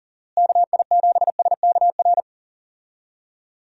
Morse code, heard as a keyed tone, about 40 wpm, 700 Hz. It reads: KI7SKR
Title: USA Call Signs - 6 Characters - Repeated 2x - 40wpm